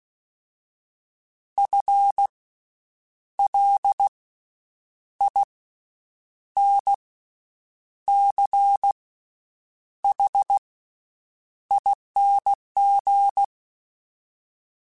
Morse code, heard as a keyed tone, about 16 wpm, spaced as for 7 wpm, 800 Hz. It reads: FLINCHING
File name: Special INGs / Special INGs-mp3s / VerySlowToFaster Ing @15